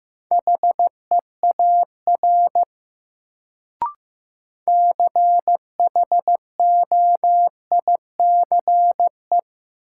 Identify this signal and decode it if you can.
Morse code, heard as a keyed tone, about 16 wpm, 700 Hz. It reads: HEAR CHOICE
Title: Sets of 5 Words - New Word Drill - 15wpm